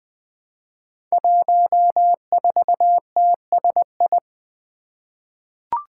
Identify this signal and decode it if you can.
Morse code, heard as a keyed tone, about 20 wpm, 700 Hz. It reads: 14TSI
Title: Five Challenging Letter-Number Combinations 20wpm